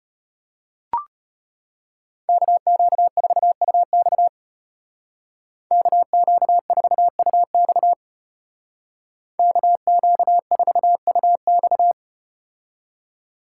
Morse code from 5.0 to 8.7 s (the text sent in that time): KQ4UX